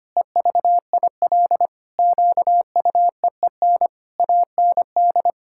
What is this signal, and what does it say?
Morse code, heard as a keyed tone, about 25 wpm, 700 Hz. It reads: EVIL QUEEN AND
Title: Pangram Challenge v2 - 25wpm